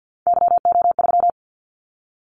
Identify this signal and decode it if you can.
Morse code, heard as a keyed tone, about 50 wpm, 700 Hz. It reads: YO3
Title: Callsign Prefixes for Top 25 DXCC - 50wpm